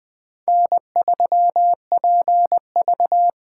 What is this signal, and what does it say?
Morse code, heard as a keyed tone, about 20 wpm, 700 Hz. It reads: N3PV
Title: Speed Racing - USA Call Signs 17wpm